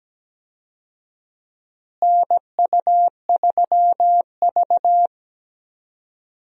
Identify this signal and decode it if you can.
Morse code, heard as a keyed tone, about 17 wpm, 700 Hz. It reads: NU3V
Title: USA Call Signs - 4 Characters - Repeated 3x - 17wpm